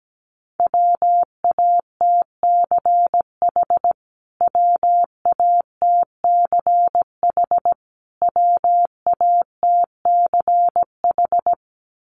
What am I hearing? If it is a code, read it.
Morse code, heard as a keyed tone, about 17 wpm, 700 Hz. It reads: WATCH WATCH WATCH
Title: Top 300 Words Repeated 17wpm